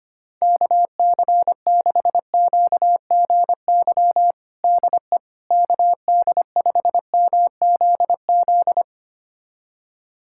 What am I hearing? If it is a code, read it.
Morse code, heard as a keyed tone, about 25 wpm, 700 Hz. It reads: KC6QGY DE KD5MZZ
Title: USA Callsign DE Callsign - 6 Characters - 25wpm